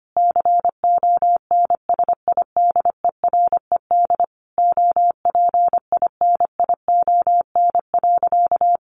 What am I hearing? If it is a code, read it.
Morse code, heard as a keyed tone, about 25 wpm, 700 Hz. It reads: CONSIDERED OPINION.
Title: A Sound Of Thunder 25wpm